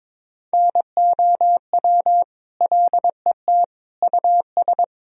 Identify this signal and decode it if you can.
Morse code, heard as a keyed tone, about 22 wpm, 700 Hz. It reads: NOW LET US
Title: Sentences from Top 1000 Words - Review Part 1 - 22wpm